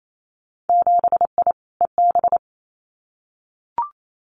Morse code, heard as a keyed tone, about 28 wpm, 700 Hz. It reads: 7I EB
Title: Winter Field Day Exchanges Encore 28wpm